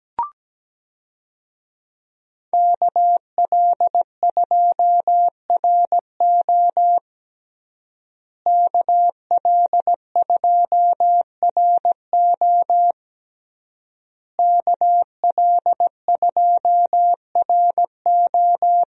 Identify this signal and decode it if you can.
Morse code, heard as a keyed tone, about 17 wpm, 700 Hz. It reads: KL2RO KL2RO KL2RO
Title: USA Call Signs - 5 Characters - Repeated 3x - 17wpm